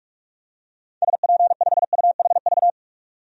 Morse code, heard as a keyed tone, about 45 wpm, 700 Hz. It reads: SP5UHV
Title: DXCC-100 Call Signs - 6 Characters - 45wpm